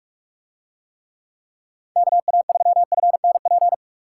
Morse code, heard as a keyed tone, about 44 wpm, 700 Hz. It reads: KA3FNP